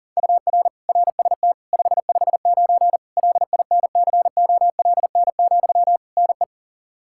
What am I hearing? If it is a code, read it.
Morse code, heard as a keyed tone, about 40 wpm, 700 Hz. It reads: UR RST 559 LINCOLN, NE
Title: Practice QSO - Exchange 1 - Beginning - 40wpm